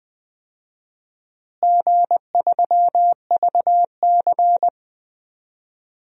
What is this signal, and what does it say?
Morse code, heard as a keyed tone, about 20 wpm, 700 Hz. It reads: G3VC
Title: DXCC-100 Call Signs - 4 Characters - Repeated 3x - 20wpm